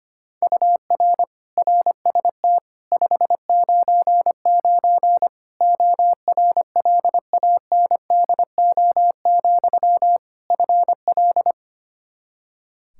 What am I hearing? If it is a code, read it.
Morse code, heard as a keyed tone, about 25 wpm, 700 Hz. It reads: UR RST 599 ORLANDO, FL
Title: Practice QSO - Exchange 1 - Beginning - 25wpm